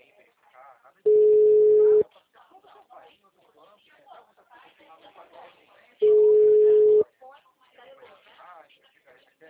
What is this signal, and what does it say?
Ringback tone (Europe, 425 Hz), with speech.